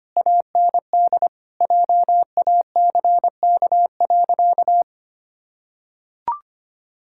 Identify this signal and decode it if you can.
Morse code, heard as a keyed tone, about 25 wpm, 700 Hz. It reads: AND JACK.
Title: Pangram Challenge v2 - 25wpm